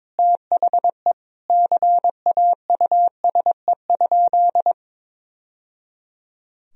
Morse code, heard as a keyed tone, about 22 wpm, 700 Hz. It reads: THE CAUSE?
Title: Sentences from Top 1000 Words - Review Part 1 - 22wpm